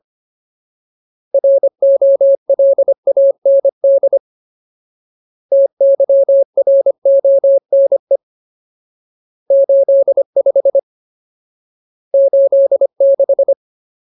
Morse code, heard as a keyed tone, about 25 wpm, 550 Hz. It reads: ROLAND TYRONE 85 86
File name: Ultimate Word List / 25WPM Ultimate